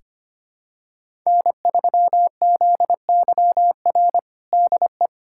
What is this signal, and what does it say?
Morse code, heard as a keyed tone, about 25 wpm, 700 Hz. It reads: N3ZYR DE
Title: USA Callsign DE Callsign - 5 Characters - 25wpm